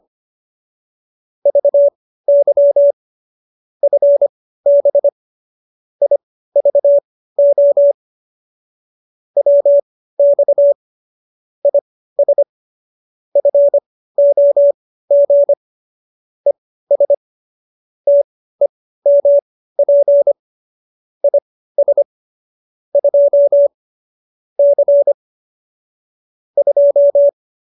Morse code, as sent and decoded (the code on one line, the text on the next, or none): ...- -.-- / ..-. -... / .. ...- --- / .-- -..- / .. ... / ..-. --- --. / . ... / - . -- .--. / .. ... / ..--- / -.-. / ..---
VY FB IVO WX IS FOG ES TEMP IS 2 C 2